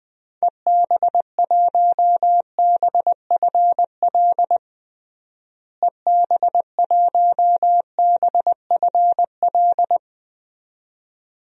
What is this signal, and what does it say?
Morse code, heard as a keyed tone, about 21 wpm, 700 Hz. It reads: EB1BFL EB1BFL
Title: DXCC-100 Call Signs - 6 Characters - Repeated 2x - 20wpm